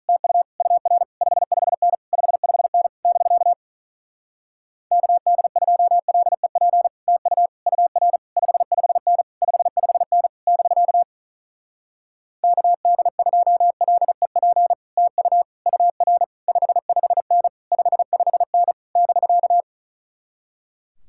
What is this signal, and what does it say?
Morse code, heard as a keyed tone, about 43 wpm, 700 Hz. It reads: TU UR 55N 55N <BK> KD2LEP TU UR 55N 55N <BK> KD2LEP TU UR 55N 55N <BK>
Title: Speed-Racing Common QSO Phrases - 35wpm